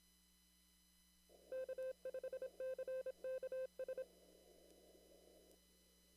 Morse code, heard as a keyed tone, about 27 wpm, 520 Hz. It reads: K5CKS
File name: CW / kt5j wpx cw-op-k5pi